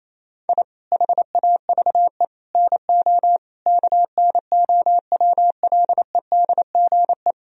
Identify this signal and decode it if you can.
Morse code, heard as a keyed tone, about 28 wpm, 700 Hz. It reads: I HAVE NO KNOWLEDGE
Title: Sentences from Top 1000 Words - Review Part 3 - 28wpm